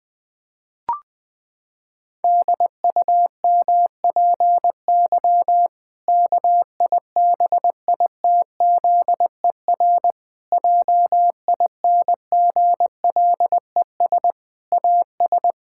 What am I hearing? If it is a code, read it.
Morse code, heard as a keyed tone, about 21 wpm, 700 Hz. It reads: DUMPY KIBITZER JINGLES AS
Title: Pangram Challenge v2 - 20wpm